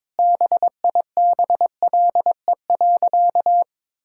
Morse code, heard as a keyed tone, about 22 wpm, 700 Hz. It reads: BIBLE.
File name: Sentences with 9-Word Phrases - 22wpm